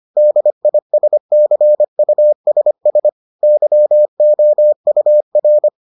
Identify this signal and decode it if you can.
Morse code, heard as a keyed tone, about 25 wpm, 600 Hz. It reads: DISCUSS YOUR